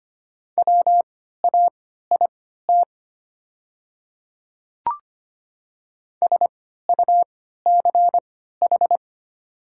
Morse code, heard as a keyed tone, about 25 wpm, spaced as for 14 wpm, 700 Hz. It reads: WAIT SUCH
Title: 3x Character Spacing - Top 500 Words - 25wpm